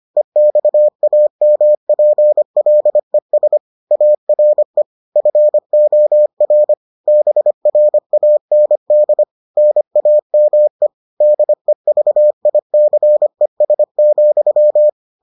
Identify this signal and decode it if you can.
Morse code, heard as a keyed tone, about 25 wpm, 600 Hz. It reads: EXAMPLES ARE FOR BRAND NAME DEVICES,